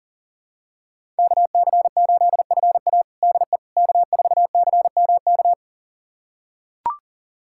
Morse code, heard as a keyed tone, about 37 wpm, 700 Hz. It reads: KC8FA DE K4CMK
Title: USA Callsign DE Callsign - 5 Characters - 40wpm